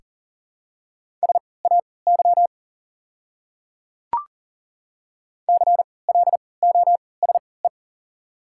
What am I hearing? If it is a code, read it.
Morse code, heard as a keyed tone, about 40 wpm, spaced as for 23 wpm, 700 Hz. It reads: SAY CLOSE
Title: 3x Character Spacing - Top 300 Words - 40wpm